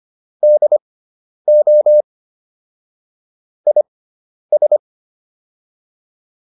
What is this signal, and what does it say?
Morse code, heard as a keyed tone, about 25 wpm, spaced as for 10 wpm, 600 Hz. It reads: DO IS
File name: WD401 10